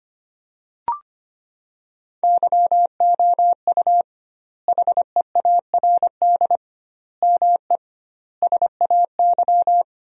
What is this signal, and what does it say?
Morse code, heard as a keyed tone, about 25 wpm, 700 Hz. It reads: YOU HEARD ME SAY
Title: 2x Word-Spacing - Sentences from Top 500 Words - 25wpm